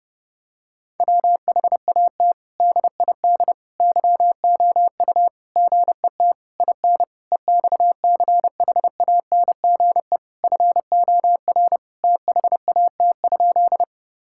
Morse code, heard as a keyed tone, about 30 wpm, 700 Hz. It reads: WHAT DID YOU GET IN EXCHANGE FOR THAT?
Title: Sentences from Top 1000 Words - Review Part 6 - 30wpm